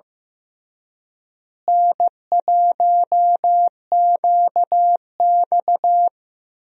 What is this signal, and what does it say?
Morse code, heard as a keyed tone, about 15 wpm, 700 Hz. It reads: N1QX
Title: USA Call Signs - 4 Characters - Repeated 3x - 15wpm